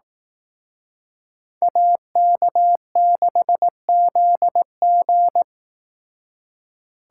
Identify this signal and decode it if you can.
Morse code, heard as a keyed tone, about 18 wpm, 700 Hz. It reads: AK6ZG
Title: Speed Racing - USA 2x2 Call Signs 15wpm